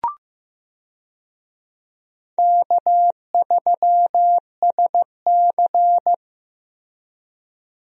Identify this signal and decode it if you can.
Morse code, heard as a keyed tone, about 15 wpm, 700 Hz. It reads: K3SC
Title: USA Call Signs - 4 Characters - Repeated 2x - 15wpm